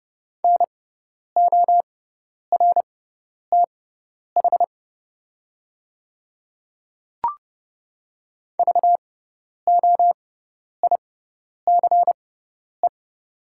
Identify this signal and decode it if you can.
Morse code, heard as a keyed tone, about 31 wpm, spaced as for 10 wpm, 690 Hz. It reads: NORTH VOICE
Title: 6x Character Spacing - Top 500 Words - 30wpm